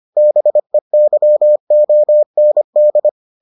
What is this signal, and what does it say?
Morse code, heard as a keyed tone, about 25 wpm, 600 Hz. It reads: BEYOND